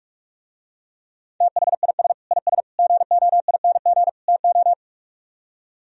Morse code, heard as a keyed tone, about 49 wpm, 700 Hz. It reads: THIS IS GOING TO